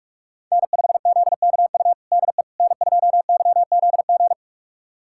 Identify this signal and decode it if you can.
Morse code, heard as a keyed tone, about 46 wpm, 700 Hz. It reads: N5ZKU DE N2YZG
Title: USA Callsign DE Callsign - 5 Characters - 45wpm